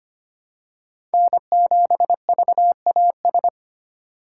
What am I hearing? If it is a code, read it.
Morse code, heard as a keyed tone, about 25 wpm, 700 Hz. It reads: N7VAS
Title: USA Call Signs - 5 Characters - 4x Character Spacing - 25wpm